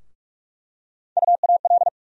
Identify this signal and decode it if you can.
Morse code, heard as a keyed tone, about 47 wpm, 700 Hz. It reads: UAL
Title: USA Call Sign Postfixes - 3 Letters 45wpm